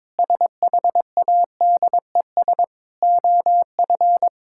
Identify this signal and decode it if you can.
Morse code, heard as a keyed tone, about 22 wpm, 700 Hz. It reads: SHADES OF
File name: Difficult Sentences with Hints - 22wpm